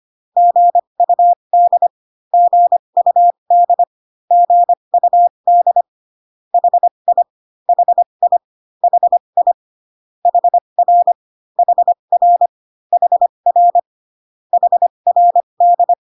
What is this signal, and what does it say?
Morse code, heard as a keyed tone, about 25 wpm, spaced as for 22 wpm, 700 Hz. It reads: GUD GUD GUD HI HI HI HR HR HR HRD
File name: CW common abbreviations alphabetical repeated 3 times 25 18wpm